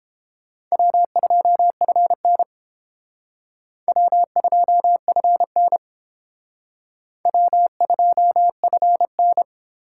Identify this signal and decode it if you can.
Morse code, heard as a keyed tone, about 30 wpm, 700 Hz. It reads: W2FN W2FN W2FN